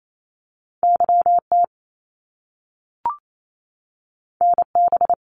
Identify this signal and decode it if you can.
Morse code, heard as a keyed tone, about 28 wpm, 700 Hz. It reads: YT NB
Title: Canadian Provinces and Territories 28wpm